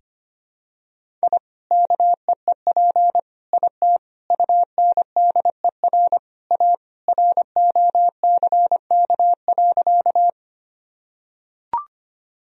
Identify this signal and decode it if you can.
Morse code, heard as a keyed tone, about 25 wpm, 700 Hz. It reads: I KEEP IT UNDER A ROCK.
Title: Sentences from Top 1000 Words - Review Part 2 - 25wpm